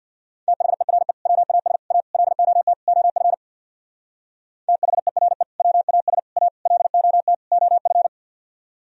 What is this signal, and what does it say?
Morse code, heard as a keyed tone, about 61 wpm, 690 Hz. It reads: THERE WAS A LOT OF THERE WAS A LOT OF
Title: Speed-Racing - Sets of 5 Words 50wpm